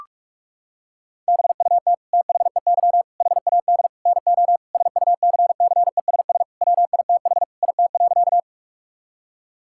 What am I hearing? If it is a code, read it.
Morse code, heard as a keyed tone, about 46 wpm, 690 Hz. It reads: BUT THEY HAD NO SUCCESS WITH IT.
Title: Sentences from Top 1000 Words - Review Part 6 - 45wpm